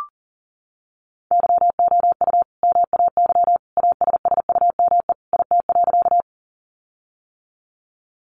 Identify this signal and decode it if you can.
Morse code, heard as a keyed tone, about 40 wpm, 700 Hz. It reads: YOU MAY ASSUME IT.
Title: Sentences from Top 1500 Words 40wpm